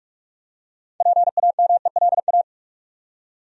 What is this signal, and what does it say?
Morse code, heard as a keyed tone, about 46 wpm, 700 Hz. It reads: PAMELA
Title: USA Common Names - 45wpm